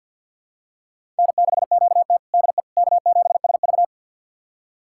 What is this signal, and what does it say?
Morse code, heard as a keyed tone, about 52 wpm, 700 Hz. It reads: N6QT DE K7SV
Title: USA Callsign DE Callsign - 4 Characters - 50wpm